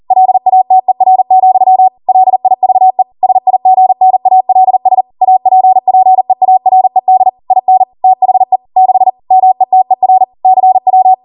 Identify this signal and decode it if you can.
Morse code, heard as a keyed tone, about 40 wpm, 750 Hz. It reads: LATER, LIVE SIGNALS APPEARED IN THE 6 METER CW